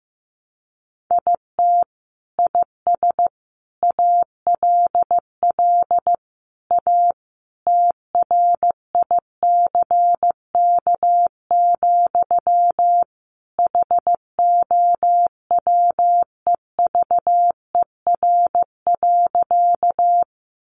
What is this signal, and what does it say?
Morse code, heard as a keyed tone, about 15 wpm, 700 Hz. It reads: IT IS ALL A TRICK, HOWEVER.
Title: Sentences from Top 1500 Words 15wpm